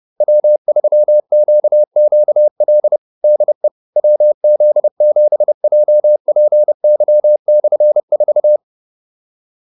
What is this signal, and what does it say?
Morse code, heard as a keyed tone, about 30 wpm, 600 Hz. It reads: W3QQL DE WZ7JPY/4